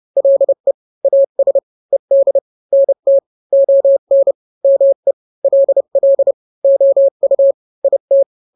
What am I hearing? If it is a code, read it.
Morse code, heard as a keyed tone, about 30 wpm, 550 Hz. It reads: LE AS ED NT ON ME LL OU IT
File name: n-grams / 2-grams 30 25 WPM